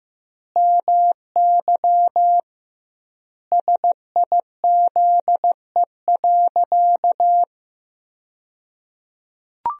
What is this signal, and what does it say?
Morse code, heard as a keyed tone, about 15 wpm, 700 Hz. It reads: MY SIZE.
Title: 2x Word-Spacing - Sentences from Top 400 Words - 15wpm